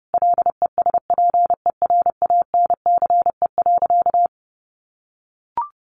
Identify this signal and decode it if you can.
Morse code, heard as a keyed tone, about 30 wpm, 700 Hz. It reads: LESPERANCE.
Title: A Sound Of Thunder 30wpm